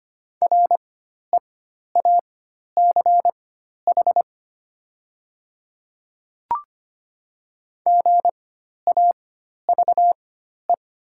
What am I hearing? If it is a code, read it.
Morse code, heard as a keyed tone, about 25 wpm, spaced as for 12 wpm, 700 Hz. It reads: REACH GAVE
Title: 4x Character Spacing - Top 500 Words - 25wpm